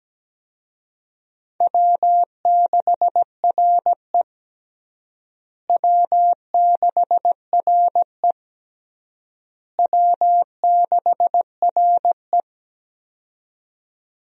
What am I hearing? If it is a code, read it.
Morse code, heard as a keyed tone, about 17 wpm, 700 Hz. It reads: W6RE W6RE W6RE
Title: USA Call Signs - 4 Characters - Repeated 3x - 17wpm